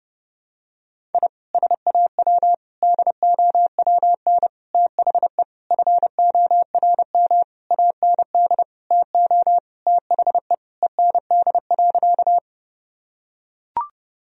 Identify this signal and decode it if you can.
Morse code, heard as a keyed tone, about 30 wpm, 700 Hz. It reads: I SAW DOWN THE FORM AND TO THE END.
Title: Sentences from Top 1000 Words - Review Part 1 - 30wpm